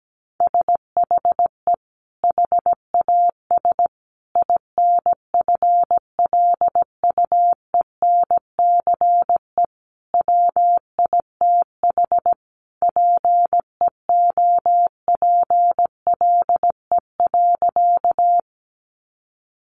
Morse code, heard as a keyed tone, about 17 wpm, 700 Hz. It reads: SHE HAS INFLUENCE WITH PEOPLE.
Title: Sentences from Top 1100 Words 17wpm